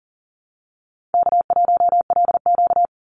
Morse code, heard as a keyed tone, about 40 wpm, 700 Hz. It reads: K1LQ
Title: USA Call Signs 40wpm